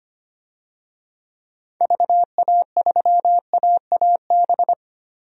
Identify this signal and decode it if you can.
Morse code, heard as a keyed tone, about 25 wpm, 700 Hz. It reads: VA3AAB
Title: DXCC-100 Call Signs - 6 Characters - Repeated 3x - 25wpm